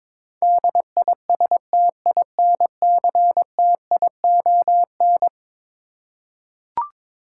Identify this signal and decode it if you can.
Morse code, heard as a keyed tone, about 21 wpm, 700 Hz. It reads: DISTINCTION
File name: practice / 22wpm / Eleven Letter Words - 22wpm